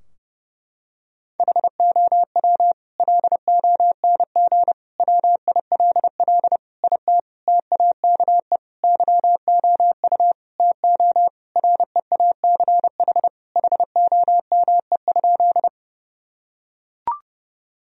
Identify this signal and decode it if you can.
Morse code, heard as a keyed tone, about 32 wpm, 700 Hz. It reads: HOW LONG WILL IT TAKE YOU TO REACH HOME?
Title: Sentences from Top 1000 Words - Review Part 2 - 30wpm